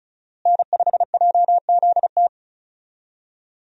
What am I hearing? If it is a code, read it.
Morse code, heard as a keyed tone, about 36 wpm, 700 Hz. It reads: N5JZT
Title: USA Call Signs - 5 Characters - Repeated 2x - 35wpm